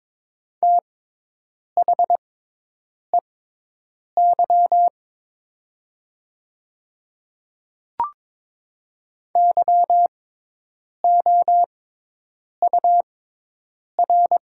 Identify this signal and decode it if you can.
Morse code, heard as a keyed tone, about 22 wpm, spaced as for 8 wpm, 700 Hz. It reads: THEY YOUR
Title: 6x Character Spacing - Top 200 Words - 22wpm